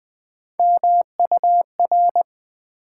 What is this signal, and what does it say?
Morse code, heard as a keyed tone, about 20 wpm, 700 Hz. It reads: MUR
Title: Build A Word - 3 Letters At A Time - 6-Letter Words - 20wpm